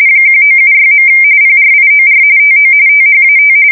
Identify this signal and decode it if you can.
RTTY, about 45.45 baud. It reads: 1976. PRIOR TO THAT
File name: fsk170 / teletype-RTTY